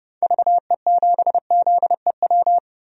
Morse code, heard as a keyed tone, about 32 wpm, 700 Hz. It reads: VE7ZEW
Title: DXCC-100 Call Signs - 6 Characters - 30wpm